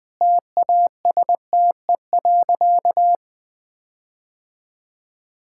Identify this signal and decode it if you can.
Morse code, heard as a keyed tone, about 20 wpm, 700 Hz. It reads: TASTE.
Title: Sentences with 5-Word Phrases - 20wpm